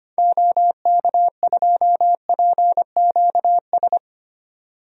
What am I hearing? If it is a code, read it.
Morse code, heard as a keyed tone, about 25 wpm, 700 Hz. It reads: OK2PQS
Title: DXCC-100 Call Signs - 6 Characters - 25wpm